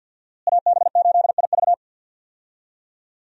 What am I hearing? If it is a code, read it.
Morse code, heard as a keyed tone, about 52 wpm, 700 Hz. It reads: AB8IV
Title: USA Call Signs - 5 Characters - Repeated 3x - 50wpm